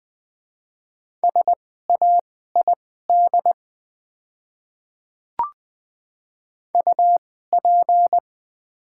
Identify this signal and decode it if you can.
Morse code, heard as a keyed tone, about 20 wpm, spaced as for 15 wpm, 700 Hz. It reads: SAID UP
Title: 2x Character Spacing - Top 100 Words - 20wpm